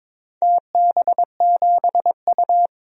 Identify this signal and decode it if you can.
Morse code, heard as a keyed tone, about 22 wpm, 700 Hz. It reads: TB7U
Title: DXCC-100 Call Signs - 4 Characters - 22wpm